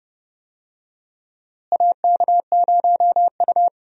Morse code, heard as a keyed tone, about 30 wpm, 700 Hz. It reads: AK0U
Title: USA Call Signs - 4 Characters - Repeated 3x - 30wpm